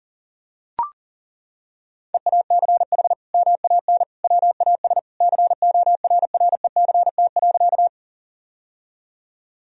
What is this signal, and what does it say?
Morse code, heard as a keyed tone, about 40 wpm, 700 Hz. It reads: EACH MAN WAS CORRECT.
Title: Sentences from Top 1000 Words - Review Part 2 - 40wpm